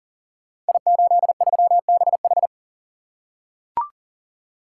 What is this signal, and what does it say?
Morse code, heard as a keyed tone, about 38 wpm, 700 Hz. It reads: I83BH
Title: Code Groups - 5 Letter:Numbers - 40wpm